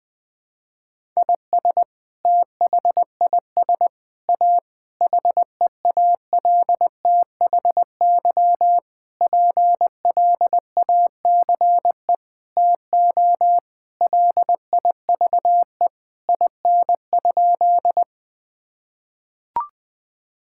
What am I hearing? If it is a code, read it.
Morse code, heard as a keyed tone, about 19 wpm, 700 Hz. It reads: IS THIS A HEALTHY PLACE TO LIVE IN?
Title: Sentences from Top 1000 Words - Review Part 5 - 20wpm